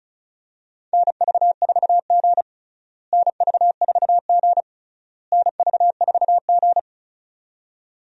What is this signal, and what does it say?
Morse code, heard as a keyed tone, about 34 wpm, 700 Hz. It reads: NV4G NV4G NV4G
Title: USA Call Signs - 4 Characters - Repeated 3x - 35wpm